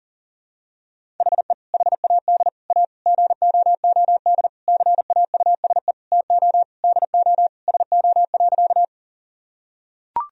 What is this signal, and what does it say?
Morse code, heard as a keyed tone, about 37 wpm, 700 Hz. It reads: HE HAD A GOOD CAUSE TO DO SO.
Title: Sentences from Top 1000 Words - Review Part 1 - 40wpm